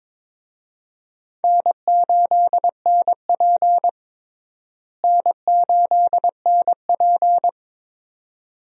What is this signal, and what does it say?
Morse code, heard as a keyed tone, about 22 wpm, 700 Hz. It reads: N8NP N8NP
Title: USA Call Signs - 4 Characters - Repeated 2x - 22wpm